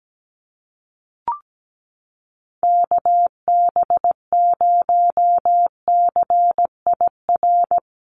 Morse code, heard as a keyed tone, about 17 wpm, 700 Hz. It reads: KB0CIR
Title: USA Call Signs Repeated 3x 17wpm